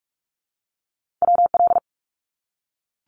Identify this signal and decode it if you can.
Morse code, heard as a keyed tone, about 44 wpm, 700 Hz. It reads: WL
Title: Abbreviations Repeated 3x - 45wpm